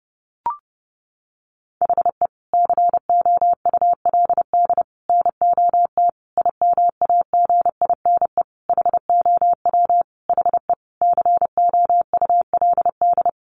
Morse code, heard as a keyed tone, about 30 wpm, 700 Hz. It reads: HE COULD NOT IMAGINE HOW HE COULD
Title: Sentences from Top 1400 Words 30wpm